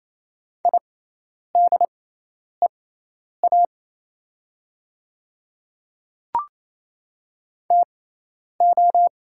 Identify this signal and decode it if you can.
Morse code, heard as a keyed tone, about 27 wpm, spaced as for 10 wpm, 700 Hz. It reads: IDEA TO
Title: 6x Character Spacing - Top 300 Words - 28wpm